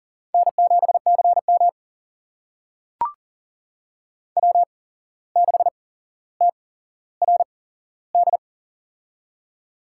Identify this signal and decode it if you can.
Morse code, heard as a keyed tone, about 40 wpm, spaced as for 16 wpm, 700 Hz. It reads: N7CM W6TRD
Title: USA Call Signs - 8x Character Spacing - 40wpm